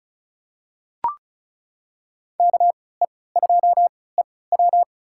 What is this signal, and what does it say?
Morse code, heard as a keyed tone, about 36 wpm, spaced as for 20 wpm, 700 Hz. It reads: KE2EW
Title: USA Call Signs - 5 Characters - 3x Character Spacing - 35wpm